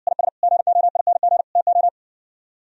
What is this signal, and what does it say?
Morse code, heard as a keyed tone, about 59 wpm, 700 Hz. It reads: IS GOING TO